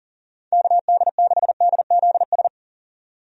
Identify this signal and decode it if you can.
Morse code, heard as a keyed tone, about 40 wpm, 700 Hz. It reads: KD6DZS
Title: USA Call Signs - 6 Characters - 40wpm